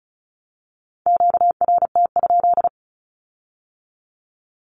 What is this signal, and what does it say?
Morse code, heard as a keyed tone, about 36 wpm, 700 Hz. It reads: QRT?
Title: Q-Codes with Spoken Meaning - 35wpm